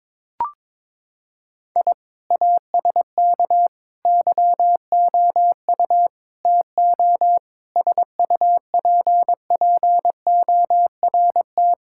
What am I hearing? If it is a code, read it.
Morse code, heard as a keyed tone, about 21 wpm, 700 Hz. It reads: I ASK YOU TO SUPPORT
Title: Sentences from Top 1000 Words - Review Part 5 - 22wpm